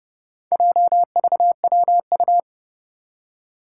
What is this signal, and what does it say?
Morse code, heard as a keyed tone, about 30 wpm, 700 Hz. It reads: JVWU
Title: Four Challenging Letter-Number Combinations 30wpm